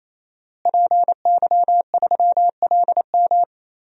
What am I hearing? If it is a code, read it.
Morse code, heard as a keyed tone, about 28 wpm, 700 Hz. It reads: PY3LM
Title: DXCC-100 Call Signs - 5 Characters - Repeated 3x - 28wpm